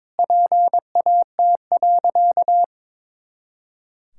Morse code, heard as a keyed tone, about 22 wpm, 700 Hz. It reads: PAT.
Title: Pangram Challenge v2 - 22wpm